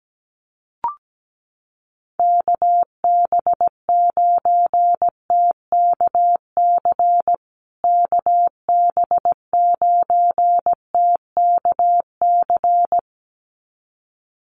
Morse code, heard as a keyed tone, about 17 wpm, 700 Hz. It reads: KB9TKC KB9TKC
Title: USA Call Signs Repeated 2x 17wpm